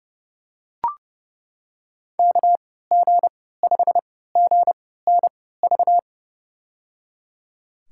Morse code, heard as a keyed tone, about 30 wpm, spaced as for 17 wpm, 700 Hz. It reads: KG5GNV
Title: USA Call Signs - 6 Characters - 3x Character Spacing - 30wpm